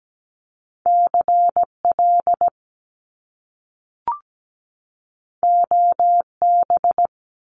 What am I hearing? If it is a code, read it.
Morse code, heard as a keyed tone, about 17 wpm, 700 Hz. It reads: CL OB
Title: Top Two Letter Combinations in Words 17wpm